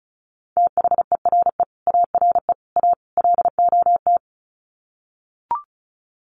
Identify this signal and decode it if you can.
Morse code, heard as a keyed tone, about 34 wpm, 700 Hz. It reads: THERE ARE A LOT
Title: Sets of 4 Words 35wpm